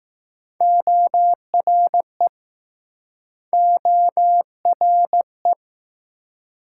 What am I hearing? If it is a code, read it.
Morse code, heard as a keyed tone, about 21 wpm, 700 Hz. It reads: ORE ORE